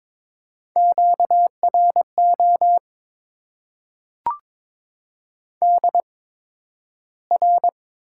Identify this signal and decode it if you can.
Morse code, heard as a keyed tone, about 22 wpm, spaced as for 12 wpm, 700 Hz. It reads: QRO DR
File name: Top 90 QSO Elements - 8x Character Spacing - 22wpm